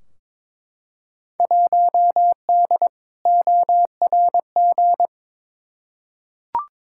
Morse code, heard as a keyed tone, about 22 wpm, 700 Hz. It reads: 1D ORG
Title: Field Day Exchanges - 22wpm